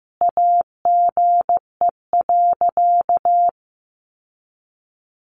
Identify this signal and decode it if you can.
Morse code, heard as a keyed tone, about 15 wpm, 700 Hz. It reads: AGE.
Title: Sentences from Top 1300 Words 15wpm